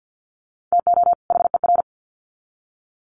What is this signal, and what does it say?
Morse code, heard as a keyed tone, about 50 wpm, 700 Hz. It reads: TO HER
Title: Sets of 2 Words 50wpm